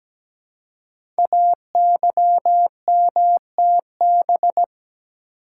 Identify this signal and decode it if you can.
Morse code, heard as a keyed tone, about 17 wpm, 700 Hz. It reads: AYMTB
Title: Code Groups - 5 Letters - 17wpm